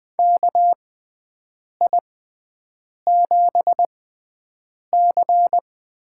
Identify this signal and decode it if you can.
Morse code, heard as a keyed tone, about 20 wpm, spaced as for 7 wpm, 700 Hz. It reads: KI7C
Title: USA Call Signs - 6 Characters - 6x Character Spacing - 20wpm